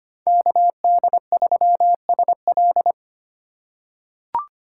Morse code, heard as a keyed tone, about 25 wpm, 700 Hz. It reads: KD3SL
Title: USA Call Signs - 5 Characters - 4x Character Spacing - 25wpm